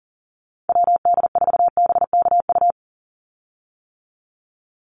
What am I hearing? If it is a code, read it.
Morse code, heard as a keyed tone, about 40 wpm, 700 Hz. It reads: WD4BKU
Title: USA Call Signs 40wpm